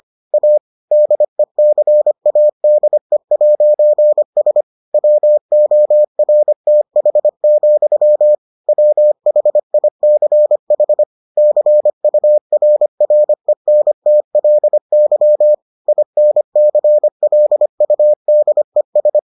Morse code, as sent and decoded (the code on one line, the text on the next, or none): .- / -.. -.-. .- -.. .----. ... / .-- --- .-. - .... --..-- / .-- .... .. -.-. .... / -.-. ..- .-. .-. -. - .-.. -.-- / .. -. -.-. .-.. ..- -.. ...
A DECADE'S WORTH, WHICH CURRENTLY INCLUDES